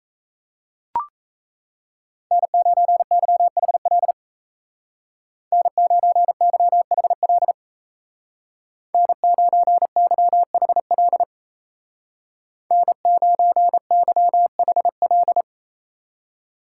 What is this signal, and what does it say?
Morse code, heard as a keyed tone, about 34 wpm, 700 Hz. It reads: N9YHL N9YHL N9YHL N9YHL
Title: Speed Racing - USA Call Signs 28wpm